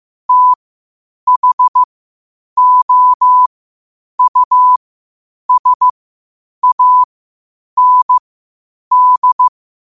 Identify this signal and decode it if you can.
Morse code, heard as a keyed tone, about 15 wpm, spaced as for 8 wpm, 1000 Hz. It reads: THOUSAND
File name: qotdfiles / 8wpm